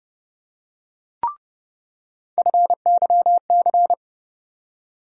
Morse code, heard as a keyed tone, about 30 wpm, 700 Hz. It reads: FYC
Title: USA Call Sign Postfixes - 3 Letters 30wpm